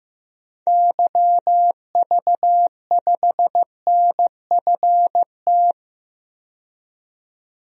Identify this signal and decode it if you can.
Morse code, heard as a keyed tone, about 16 wpm, 700 Hz. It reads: YV5NFT
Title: DXCC-100 Call Signs - 6 Characters - 15wpm